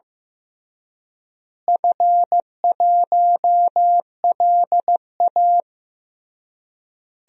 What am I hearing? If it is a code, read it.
Morse code, heard as a keyed tone, about 15 wpm, 700 Hz. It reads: F1LA
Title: DXCC-100 Call Signs - 4 Characters - Repeated 3x - 15wpm